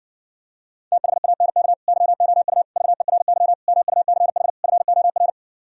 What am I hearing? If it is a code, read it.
Morse code, heard as a keyed tone, about 69 wpm, 700 Hz. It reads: THANK YOU VERY MUCH FOR